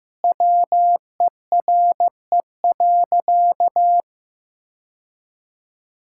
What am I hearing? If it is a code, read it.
Morse code, heard as a keyed tone, about 15 wpm, 700 Hz. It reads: WERE.
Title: Sentences from Top 1000 Words - Review Part 6 - 15wpm